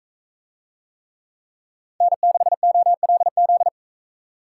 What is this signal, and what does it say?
Morse code, heard as a keyed tone, about 44 wpm, 700 Hz. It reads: N6OLZ